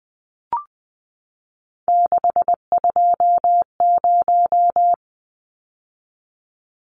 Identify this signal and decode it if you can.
Morse code, heard as a keyed tone, about 20 wpm, 700 Hz. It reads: 620
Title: Numbers - 3-Digit 20wpm